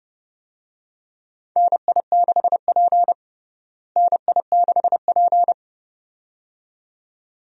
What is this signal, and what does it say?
Morse code, heard as a keyed tone, about 31 wpm, 700 Hz. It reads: NI6P NI6P
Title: USA Call Signs - 4 Characters - Repeated 2x - 30wpm